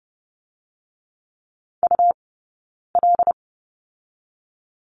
Morse code, heard as a keyed tone, about 30 wpm, 700 Hz. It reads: U L
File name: Lesson 20d - Number 3 - Two Characters 30wpm